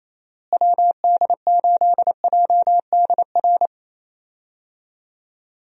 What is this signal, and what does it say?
Morse code, heard as a keyed tone, about 28 wpm, 700 Hz. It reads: WD8JDR
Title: USA Call Signs - 6 Characters - Repeated 3x - 28wpm